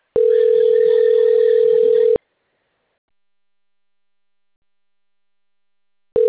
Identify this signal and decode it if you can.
Ringback tone (North America), 440 + 480 Hz.